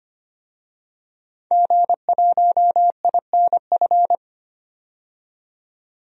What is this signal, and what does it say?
Morse code, heard as a keyed tone, about 25 wpm, 700 Hz. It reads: G1INF
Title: DXCC-100 Call Signs - 5 Characters - 25wpm